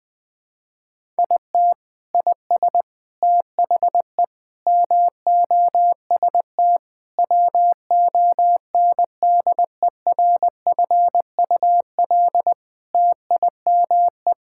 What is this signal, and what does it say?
Morse code, heard as a keyed tone, about 20 wpm, 700 Hz. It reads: IT IS THE MOST WONDERFUL TIME
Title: Sentences with 8-Word Phrases - 20wpm